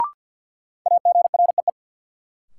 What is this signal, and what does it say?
Morse code, heard as a keyed tone, about 50 wpm, 700 Hz. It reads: AGREE
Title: Sentences from Top 1000 Words - Review Part 6 - New Word Drill - 50wpm